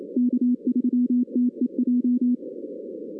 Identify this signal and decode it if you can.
Morse code, heard as a keyed tone, about 27 wpm, 250 Hz. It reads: K3TEJ
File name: 2016 arrl dx cw / 2016-ARRL-DX-CW-K5TR- - 0-41